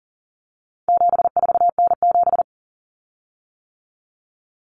Morse code, heard as a keyed tone, about 40 wpm, 700 Hz. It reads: 74N7
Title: Common Cut Numbers - 4-Digit 40wpm